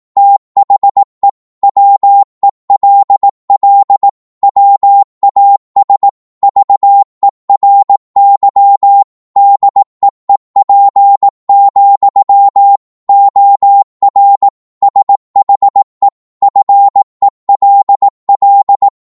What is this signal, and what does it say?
Morse code, heard as a keyed tone, about 18 wpm, 800 Hz. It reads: THE WELL WAS VERY DEEP, OR SHE FELL
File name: Morse / psg-18wpm